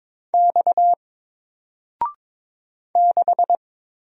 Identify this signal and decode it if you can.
Morse code, heard as a keyed tone, about 22 wpm, 700 Hz. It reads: X 6
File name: Single-Letter-Number-22wpm